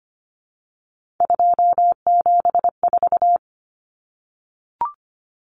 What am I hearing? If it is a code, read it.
Morse code, heard as a keyed tone, about 25 wpm, 700 Hz. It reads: 274
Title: Numbers - 3-Digit 25wpm